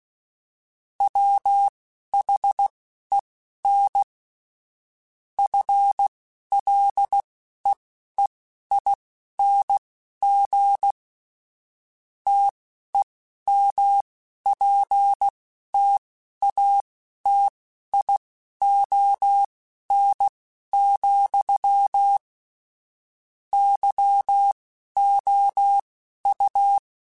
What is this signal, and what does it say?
Morse code, heard as a keyed tone, about 16 wpm, spaced as for 11 wpm, 800 Hz. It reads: WHEN FLEEING TEMPTATION, YOU